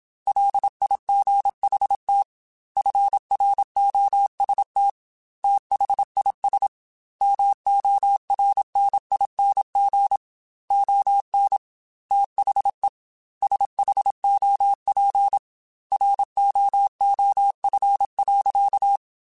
Morse code, heard as a keyed tone, about 27 wpm, 800 Hz. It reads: LIGHT FROST THIS MORNING ON THE SHOP ROOF.